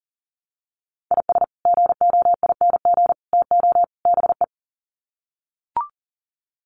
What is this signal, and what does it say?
Morse code, heard as a keyed tone, about 40 wpm, 700 Hz. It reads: IS GOING TO BE
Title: Sets of 4 Words 40wpm